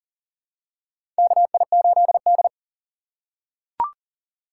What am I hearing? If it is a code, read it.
Morse code, heard as a keyed tone, about 40 wpm, 700 Hz. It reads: KI8D
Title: USA Call Signs - 4 Characters - 3x Character Spacing - 40wpm